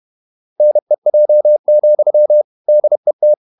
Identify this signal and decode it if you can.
Morse code, heard as a keyed tone, about 31 wpm, 600 Hz. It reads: NEJ, DET
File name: Ferie QRQ 31wpm